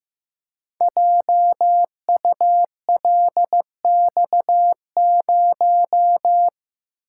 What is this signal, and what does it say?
Morse code, heard as a keyed tone, about 15 wpm, 700 Hz. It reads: JULX0
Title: Code Groups - 5 Letter:Numbers - 15wpm